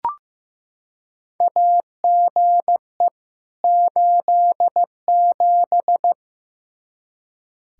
Morse code, heard as a keyed tone, about 15 wpm, 700 Hz. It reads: AGE 87
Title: K1USN SST QSO Practice 15wpm